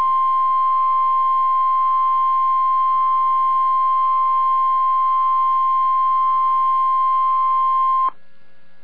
Emergency Alert System (SAME) tones (the 1050 Hz attention tone). It is heard from 0.5 to 8.1 s.